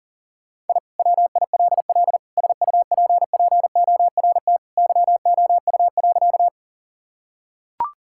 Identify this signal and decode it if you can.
Morse code, heard as a keyed tone, about 40 wpm, 700 Hz. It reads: I WILL SUPPORT YOU.
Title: Sentences from Top 1000 Words - Review Part 5 - 40wpm